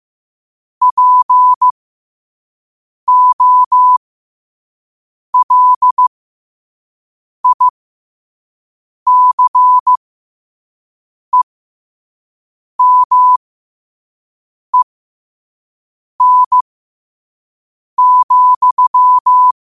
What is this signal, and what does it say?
Morse code, heard as a keyed tone, about 15 wpm, spaced as for 5 wpm, 1000 Hz. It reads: POLICEMEN,
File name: qotdfiles / 5wpm